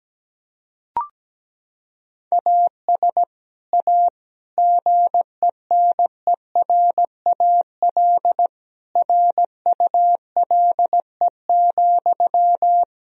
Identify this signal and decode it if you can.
Morse code, heard as a keyed tone, about 17 wpm, 700 Hz. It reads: AS A GENERAL RULE,
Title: Sentences from Top 1000 Words - Review Part 3 - 17wpm